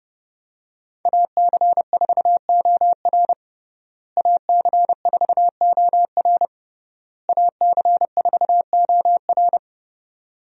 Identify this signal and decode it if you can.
Morse code, heard as a keyed tone, about 31 wpm, 700 Hz. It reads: AC4OR AC4OR AC4OR
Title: USA Call Signs - 5 Characters - Repeated 3x - 30wpm